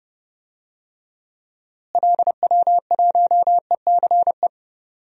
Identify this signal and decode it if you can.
Morse code, heard as a keyed tone, about 29 wpm, 700 Hz. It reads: LW1ECE
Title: DXCC-100 Call Signs - 6 Characters - Repeated 3x - 30wpm